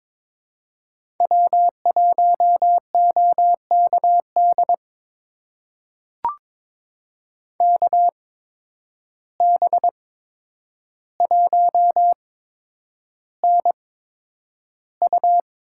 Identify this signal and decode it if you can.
Morse code, heard as a keyed tone, about 22 wpm, spaced as for 9 wpm, 700 Hz. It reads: W1OKD KB1NU
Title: USA Call Signs - 5 Characters - 8x Character Spacing - 22wpm